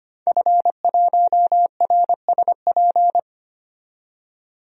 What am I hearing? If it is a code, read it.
Morse code, heard as a keyed tone, about 25 wpm, 700 Hz. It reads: F1RSP
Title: DXCC-100 Call Signs - 5 Characters - 25wpm